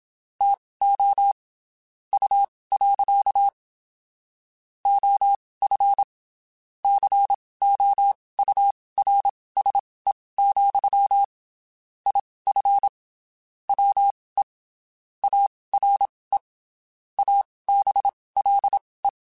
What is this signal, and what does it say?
Morse code, heard as a keyed tone, about 27 wpm, spaced as for 19 wpm, 800 Hz. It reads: TO U. OF COURSE, IF WE ARE ABLE
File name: (spaced) / QSOs Spaced-mp3s / WA7QOX 1 @25 sp